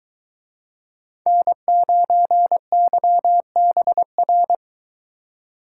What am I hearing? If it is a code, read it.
Morse code, heard as a keyed tone, about 23 wpm, 700 Hz. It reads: N9YBR